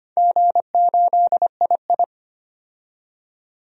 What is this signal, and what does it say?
Morse code, heard as a keyed tone, about 25 wpm, 700 Hz. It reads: G8II
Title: DXCC-100 Call Signs - 4 Characters - 25wpm